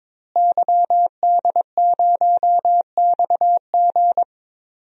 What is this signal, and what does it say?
Morse code, heard as a keyed tone, about 21 wpm, 700 Hz. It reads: YD0XG
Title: DXCC-100 Call Signs - 5 Characters - Repeated 2x - 22wpm